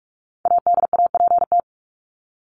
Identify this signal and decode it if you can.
Morse code, heard as a keyed tone, about 44 wpm, 700 Hz. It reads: ADAPT
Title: Top 1501-1750 Words 45wpm